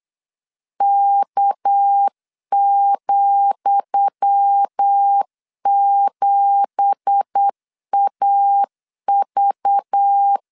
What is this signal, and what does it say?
Morse code, heard as a keyed tone, about 8 wpm, 800 Hz. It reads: K,7AV
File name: LEZIONI / 10 SIGNS / 10D COMA 8wpm 3749